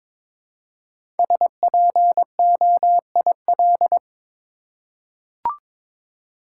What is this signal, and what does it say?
Morse code, heard as a keyed tone, about 22 wpm, 700 Hz. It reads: SPOIL
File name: 5 Letter Words - First 3 Letters Given - 22wpm